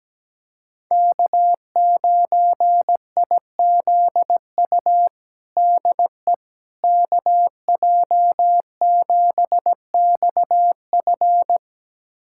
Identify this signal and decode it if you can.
Morse code, heard as a keyed tone, about 17 wpm, 700 Hz. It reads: K9IZU DE KJ7XF
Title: USA Callsign DE Callsign - 5 Characters - 17wpm